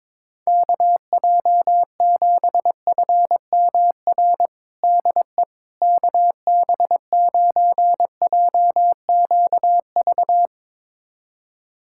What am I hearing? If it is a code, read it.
Morse code, heard as a keyed tone, about 22 wpm, 700 Hz. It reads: KJ7FMR DE KB9JQV
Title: USA Callsign DE Callsign 22wpm